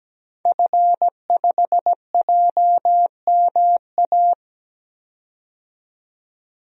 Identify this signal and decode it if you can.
Morse code, heard as a keyed tone, about 17 wpm, 700 Hz. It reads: F5JMA
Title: DXCC-100 Call Signs - 5 Characters - 17wpm